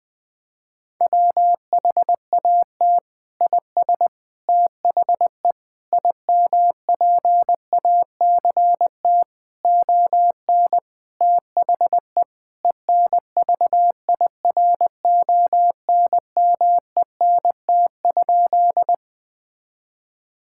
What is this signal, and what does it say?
Morse code, heard as a keyed tone, about 20 wpm, 700 Hz. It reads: WHAT IS THE IMPACT ON THE ENVIRONMENT?
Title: Sentences from Top 1000 Words - Review Part 6 - 20wpm